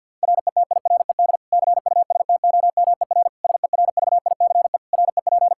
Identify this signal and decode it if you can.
Morse code, heard as a keyed tone, about 52 wpm, 700 Hz. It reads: RETIRED CUSTOMER SERVICE REP
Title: Practice QSO - Exchange 3 - More Info - 50wpm